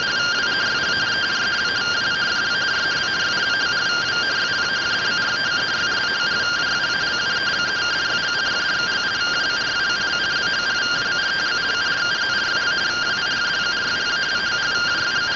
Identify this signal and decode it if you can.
Two-tone teleprinter signal, about 170 Hz shift.